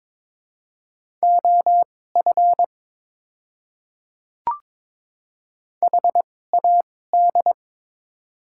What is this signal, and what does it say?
Morse code, heard as a keyed tone, about 22 wpm, spaced as for 16 wpm, 700 Hz. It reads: OF HAD
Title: 2x Character Spacing - Top 100 Words - 22wpm